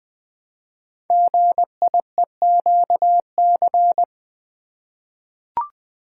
Morse code, heard as a keyed tone, about 20 wpm, 700 Hz. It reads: GIEQC
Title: Code Groups - 5 Letter:Numbers - 20wpm